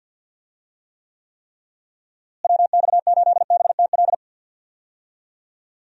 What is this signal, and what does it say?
Morse code, heard as a keyed tone, about 51 wpm, 700 Hz. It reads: WX8BTL
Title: USA Call Signs - 6 Characters - Repeated 3x - 50wpm